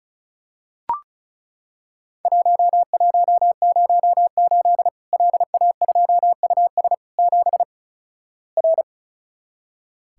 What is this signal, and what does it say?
Morse code, heard as a keyed tone, about 36 wpm, 700 Hz. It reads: 1108 LA2US 7 R
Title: Worked All Europe DX Contest - QTC Exchange - 35wpm